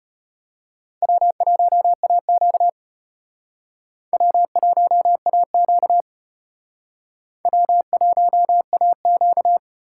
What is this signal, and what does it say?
Morse code, heard as a keyed tone, about 34 wpm, 700 Hz. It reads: W1AQ W1AQ W1AQ